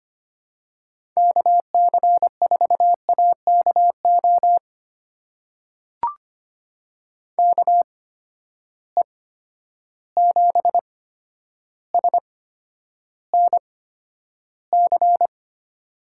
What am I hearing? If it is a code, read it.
Morse code, heard as a keyed tone, about 25 wpm, spaced as for 11 wpm, 700 Hz. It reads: KC4AKO KE7SNC
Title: USA Call Signs - 6 Characters - 8x Character Spacing - 25wpm